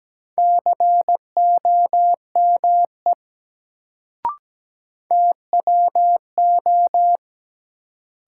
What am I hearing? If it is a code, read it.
Morse code, heard as a keyed tone, about 17 wpm, 700 Hz. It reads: COME TWO
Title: Sentences from Top 100 Words - New Word Drill - 17wpm